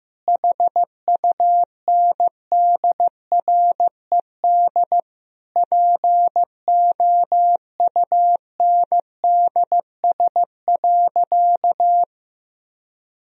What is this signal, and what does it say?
Morse code, heard as a keyed tone, about 15 wpm, 700 Hz. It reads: HUNDRED POUNDS.
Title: Sentences from Top 1000 Words - Review Part 2 - 15wpm